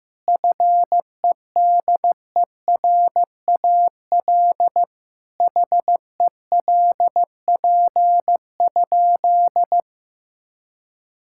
Morse code, heard as a keyed tone, about 15 wpm, 700 Hz. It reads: FEDERAL HELP?
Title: Sentences from Top 1000 Words - Review Part 5 - 15wpm